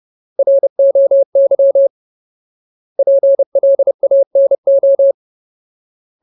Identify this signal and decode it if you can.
Morse code, heard as a keyed tone, about 30 wpm, 550 Hz. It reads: ROY PLANO
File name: Ultimate Word List / 30WPM Ultimate